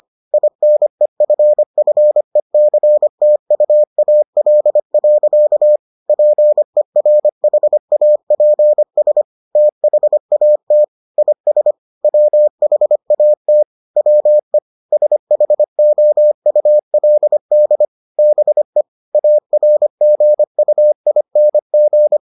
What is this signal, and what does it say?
Morse code, heard as a keyed tone, about 25 wpm, 600 Hz. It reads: INEFFECTUAL. PERHAPS THAT IS WHAT WE SHOULD BE ARGUING